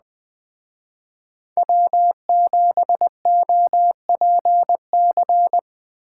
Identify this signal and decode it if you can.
Morse code, heard as a keyed tone, about 20 wpm, 700 Hz. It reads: W7OPC